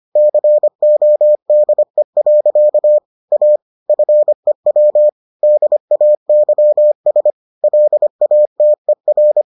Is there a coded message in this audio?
Yes: Morse code from 0.2 to 9.4 s, heard as a keyed tone, about 25 wpm, 600 Hz. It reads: CODE. A FEW DAYS LATER